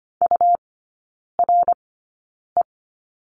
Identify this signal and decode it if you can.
Morse code, heard as a keyed tone, about 25 wpm, spaced as for 9 wpm, 700 Hz. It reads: URE
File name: Lesson 17e - Letter u - Three Characters 25wpm